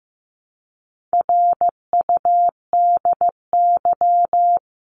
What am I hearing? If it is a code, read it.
Morse code, heard as a keyed tone, about 15 wpm, 700 Hz. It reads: RUDY
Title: CWT Exchange 15wpm